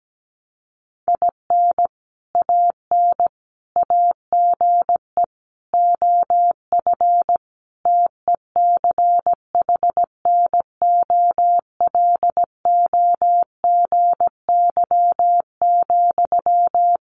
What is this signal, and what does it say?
Morse code, heard as a keyed tone, about 17 wpm, 700 Hz. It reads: IN AN AGE OF TECHNOLOGY,
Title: Sentences from Top 1200 Words 17wpm